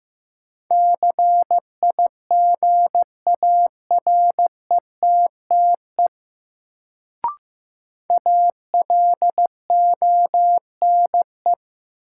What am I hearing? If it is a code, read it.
Morse code, heard as a keyed tone, about 15 wpm, 700 Hz. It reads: CIGARETTE ALONE
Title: Sentences from Top 1000 Words - New Word Drill - 15wpm